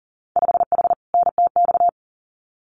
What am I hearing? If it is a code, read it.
Morse code, heard as a keyed tone, about 40 wpm, 700 Hz. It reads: 5H NTX
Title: Winter Field Day Exchanges Encore 40wpm